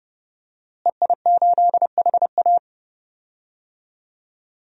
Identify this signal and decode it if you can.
Morse code, heard as a keyed tone, about 31 wpm, 700 Hz. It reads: EI8HA
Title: DXCC-100 Call Signs - 5 Characters - Repeated 3x - 30wpm